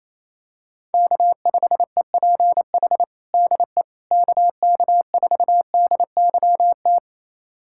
Morse code, heard as a keyed tone, about 28 wpm, 700 Hz. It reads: K5EPH DE KK4DYT
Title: USA Callsign DE Callsign 28wpm